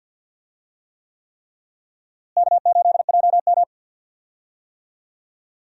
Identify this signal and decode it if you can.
Morse code, heard as a keyed tone, about 50 wpm, 700 Hz. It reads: K8JM
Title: USA Call Signs - 4 Characters - Repeated 3x - 50wpm